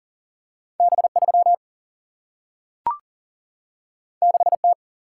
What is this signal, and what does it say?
Morse code, heard as a keyed tone, about 40 wpm, 700 Hz. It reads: B3 6T
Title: Single-Letter Single-Number 40wpm